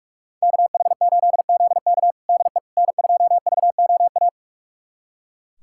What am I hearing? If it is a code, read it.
Morse code, heard as a keyed tone, about 41 wpm, 700 Hz. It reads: KH8ZK DE N2VOA
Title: USA Callsign DE Callsign - 5 Characters - 45wpm